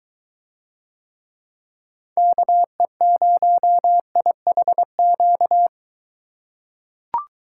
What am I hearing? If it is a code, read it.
Morse code, heard as a keyed tone, about 22 wpm, 700 Hz. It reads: KE0IHQ